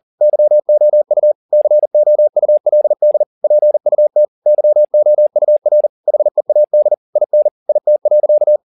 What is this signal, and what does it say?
Morse code, heard as a keyed tone, about 40 wpm, 600 Hz. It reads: YOU COULD PUT YOUR HEAD IN IT.
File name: the old man and the sea 2 40wpm